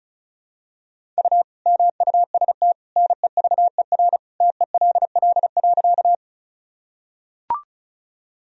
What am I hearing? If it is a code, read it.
Morse code, heard as a keyed tone, about 36 wpm, 700 Hz. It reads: U MUST NEVER TELL.
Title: Sentences from Top 300 Words and Abbreviations 35wpm